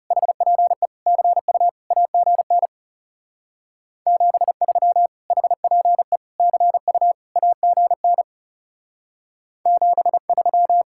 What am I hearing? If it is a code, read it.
Morse code, heard as a keyed tone, about 35 wpm, 700 Hz. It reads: HPE CU AGN 73 HPE CU AGN 73
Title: Speed-Racing Common QSO Phrases - 30wpm